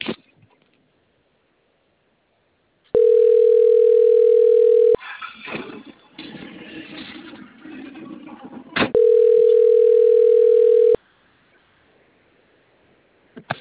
Ringback tone (North America), 440 + 480 Hz.
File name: out- - - - - .27221